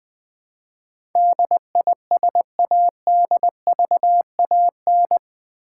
Morse code, heard as a keyed tone, about 20 wpm, 700 Hz. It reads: DISADVAN
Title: Build A Word - 4 Letters At A Time - 12-Letter Words - 20wpm